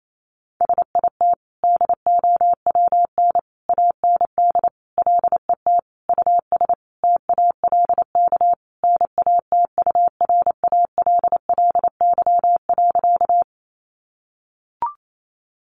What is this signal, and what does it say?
Morse code, heard as a keyed tone, about 28 wpm, 700 Hz. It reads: SIT DOWN AND LET US TALK NATURALLY.
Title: Sentences from Top 1300 Words 28wpm